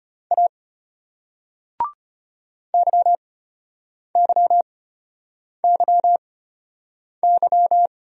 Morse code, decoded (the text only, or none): A YYYY